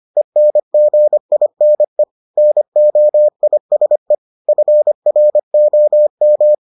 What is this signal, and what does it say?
Morse code, heard as a keyed tone, about 25 wpm, 600 Hz. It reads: ENGINE NOISE FROM